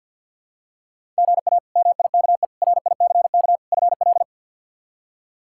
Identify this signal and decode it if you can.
Morse code, heard as a keyed tone, about 51 wpm, 700 Hz. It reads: GA MIKE RICK FL
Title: K1USN SST QSO Practice - 50wpm